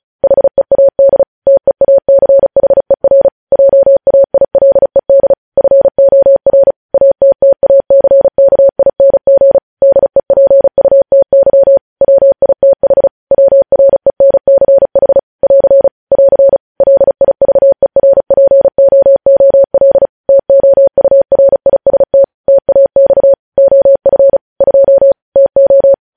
Morse code, heard as a keyed tone, about 36 wpm, 550 Hz. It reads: HEAD TEACHER JAILED FOR ATTACKING DEPUTY WITH WRENCH <AR> <AR> LIVERPOOL TOURIST TAX OF 2 TO
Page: News headlines in Morse code at 35 wpm